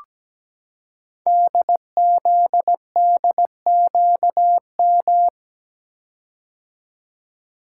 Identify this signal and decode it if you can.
Morse code, heard as a keyed tone, about 17 wpm, 700 Hz. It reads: DZDQM
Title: Code Groups - 5 Letters - 17wpm